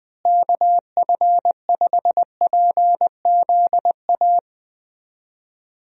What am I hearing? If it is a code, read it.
Morse code, heard as a keyed tone, about 20 wpm, 700 Hz. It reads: KF5PZA
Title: USA Call Signs - 6 Characters - Repeated 3x - 20wpm